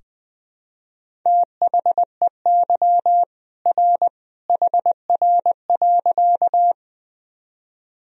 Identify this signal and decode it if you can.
Morse code, heard as a keyed tone, about 20 wpm, 700 Hz. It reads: THEY R HR.
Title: Sentences from Top 200 Words and Abbreviations 20wpm